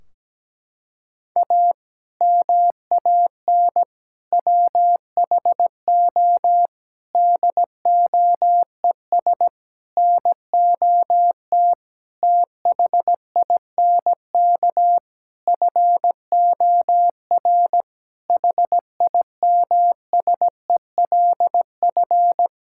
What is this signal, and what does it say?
Morse code, heard as a keyed tone, about 17 wpm, 700 Hz. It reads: A MAN WHO DOES NOT THINK FOR HIMSELF
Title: Sentences from Top 1600 Words 17wpm